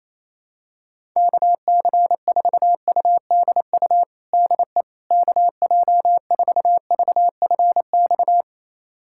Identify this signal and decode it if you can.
Morse code, heard as a keyed tone, about 28 wpm, 700 Hz. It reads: KC4UDU DE KJ4VFX
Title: USA Callsign DE Callsign - 6 Characters - 28wpm